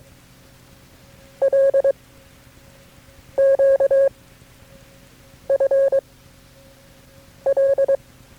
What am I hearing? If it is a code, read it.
Morse code, heard as a keyed tone, about 22 wpm, 540 Hz. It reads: L Q F L
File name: FreeFiles / CW-practice / KB6MT-tape3-side1